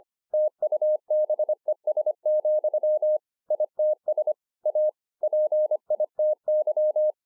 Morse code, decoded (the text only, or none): TUBES, ITS A PITY